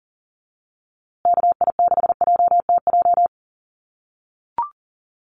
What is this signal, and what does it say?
Morse code, heard as a keyed tone, about 39 wpm, 700 Hz. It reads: KI6JTJ
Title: USA Call Signs Repeated 2x 40wpm